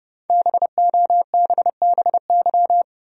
Morse code, heard as a keyed tone, about 30 wpm, 700 Hz. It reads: BOBBY
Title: USA Common Names - Max 5 Letters - 30wpm